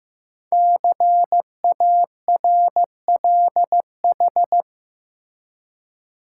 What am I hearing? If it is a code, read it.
Morse code, heard as a keyed tone, about 15 wpm, 700 Hz. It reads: CARLH
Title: Code Groups - 5 Letters - 15wpm